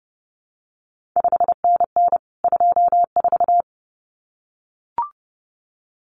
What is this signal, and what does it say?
Morse code, heard as a keyed tone, about 30 wpm, 700 Hz. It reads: 5NN 24
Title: CQ WW DX Contest Exchange - 30wpm